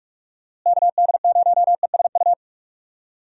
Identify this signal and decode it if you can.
Morse code, heard as a keyed tone, about 46 wpm, 700 Hz. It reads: KD0ESU
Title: USA Call Signs - 6 Characters - Repeated 3x - 45wpm